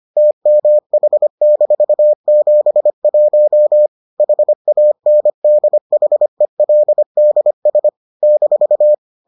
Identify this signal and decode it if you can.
Morse code, heard as a keyed tone, about 25 wpm, 600 Hz. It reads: TMH-71 HANDHELDS -